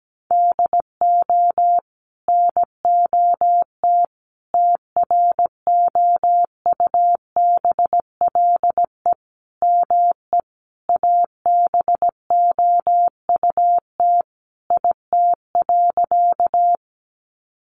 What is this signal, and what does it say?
Morse code, heard as a keyed tone, about 17 wpm, 700 Hz. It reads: DO NOT TROUBLE ME ABOUT IT.
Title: Sentences from Top 1200 Words 17wpm